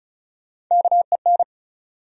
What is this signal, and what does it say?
Morse code, heard as a keyed tone, about 36 wpm, 700 Hz. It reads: KEN
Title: USA Common Names - Max 4 Letters - 35wpm